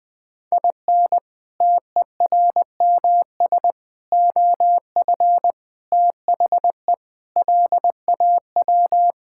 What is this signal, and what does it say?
Morse code, heard as a keyed tone, about 20 wpm, 700 Hz. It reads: IN TERMS OF THE LAW
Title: Sentences with 4-Word Phrases - 20wpm